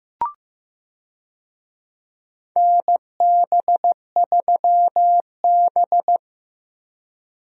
Morse code, heard as a keyed tone, about 16 wpm, 700 Hz. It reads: NB3B
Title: USA Call Signs - 4 Characters - Repeated 3x - 15wpm